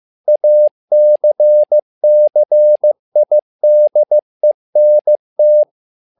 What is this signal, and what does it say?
Morse code, heard as a keyed tone, about 15 wpm, 600 Hz. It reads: ACCIDENT
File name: learning center / tower trouble / tower trouble 15wpm